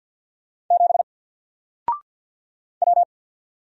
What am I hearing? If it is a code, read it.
Morse code, heard as a keyed tone, about 48 wpm, 700 Hz. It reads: 7 W
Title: Single Letter-Number 50wpm